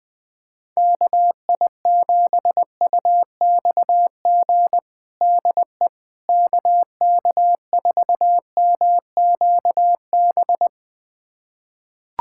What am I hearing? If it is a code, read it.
Morse code, heard as a keyed tone, about 20 wpm, 700 Hz. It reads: KI7UXG DE KK4MQB
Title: USA Callsign DE Callsign - 6 Characters - 20wpm